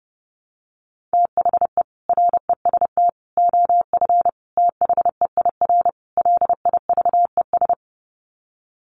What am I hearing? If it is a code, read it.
Morse code, heard as a keyed tone, about 30 wpm, 700 Hz. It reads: THE REST OF THEIR LIVES
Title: Sets of 5 Words Encore 30wpm